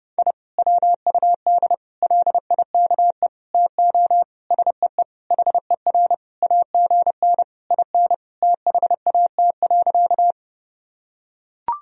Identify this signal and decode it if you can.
Morse code, heard as a keyed tone, about 30 wpm, 700 Hz. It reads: I WUD LIKE TO SEE HER AGN IN THAT.
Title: Sentences from Top 200 Words and Abbreviations 30wpm